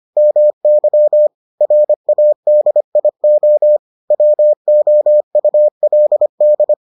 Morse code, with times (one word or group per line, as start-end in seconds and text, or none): 0.2-1.3 MY
1.6-3.8 RADIO
4.1-6.7 WOULD